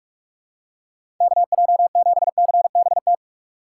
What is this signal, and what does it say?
Morse code, heard as a keyed tone, about 41 wpm, 700 Hz. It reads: KJ7CBT
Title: USA Call Signs - 6x Character Spacing - 45wpm